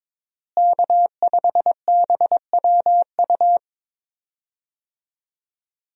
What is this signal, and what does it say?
Morse code, heard as a keyed tone, about 22 wpm, 700 Hz. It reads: K5BWU
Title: USA Call Signs - 5 Characters - Repeated 2x - 22wpm